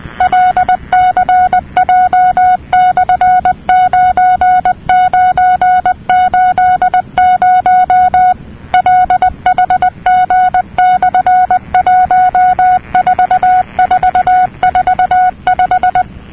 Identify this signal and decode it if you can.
Morse code, heard as a keyed tone, about 20 wpm, 730 Hz. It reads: LCJ/9980 LHG/14445
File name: europe / nor / nor-237-ptt jeloey'lcj,...'-cw-(1985)-(rb)-(mc-909-2-10)